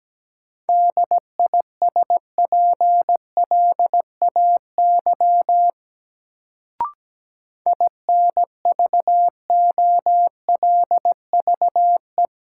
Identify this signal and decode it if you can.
Morse code, heard as a keyed tone, about 17 wpm, 700 Hz. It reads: DISPLAY INVOLVE
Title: Sentences from Top 1200 Words - New Word Drill - 17wpm